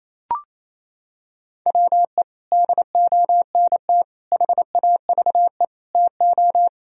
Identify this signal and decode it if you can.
Morse code, heard as a keyed tone, about 28 wpm, 700 Hz. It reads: WE DONT HAVE TO
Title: Sentences from Top 1000 Words - Review Part 2 - 28wpm